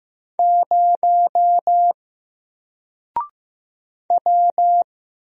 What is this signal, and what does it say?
Morse code, heard as a keyed tone, about 15 wpm, 700 Hz. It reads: 0 W
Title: Single Character (Letter:Number:Punctuation:Prosign) Alt 15wpm